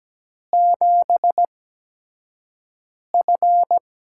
Morse code, heard as a keyed tone, about 17 wpm, spaced as for 5 wpm, 700 Hz. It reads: 7F
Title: USA Call Signs - 4 Characters - 8x Character Spacing - 17wpm